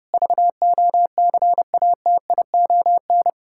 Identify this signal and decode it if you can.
Morse code, heard as a keyed tone, about 32 wpm, 700 Hz. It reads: VOCATION
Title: Pangram Challenge v2 - 30wpm